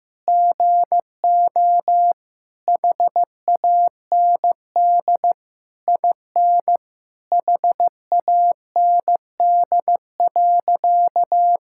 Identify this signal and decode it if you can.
Morse code, heard as a keyed tone, about 15 wpm, 700 Hz. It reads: GO HAND IN HAND.
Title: Sentences from Top 1000 Words - Review Part 6 - 15wpm